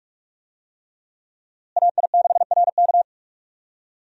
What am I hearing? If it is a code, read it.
Morse code, heard as a keyed tone, about 42 wpm, 700 Hz. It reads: AI6RK
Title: USA Call Signs - 5 Characters - Repeated 3x - 45wpm